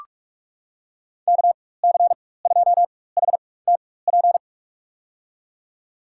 Morse code, heard as a keyed tone, about 46 wpm, spaced as for 21 wpm, 700 Hz. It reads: KC2HTP
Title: USA Call Signs - 6 Characters - 4x Character Spacing - 45wpm